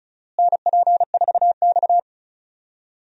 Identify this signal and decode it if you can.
Morse code, heard as a keyed tone, about 34 wpm, 700 Hz. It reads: NP4X
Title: USA Call Signs - 4 Characters - Repeated 3x - 35wpm